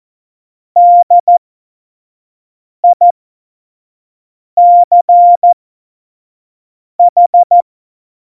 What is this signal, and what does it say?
Morse code, heard as a keyed tone, about 14 wpm, spaced as for 5 wpm, 700 Hz. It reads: DICH
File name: Morsekurs / Witze / Witz 07 April 2020, 25 F70